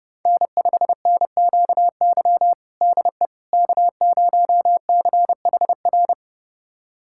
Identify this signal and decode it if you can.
Morse code, heard as a keyed tone, about 32 wpm, 700 Hz. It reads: N5NQY DE K0CHR
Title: USA Callsign DE Callsign - 5 Characters - 30wpm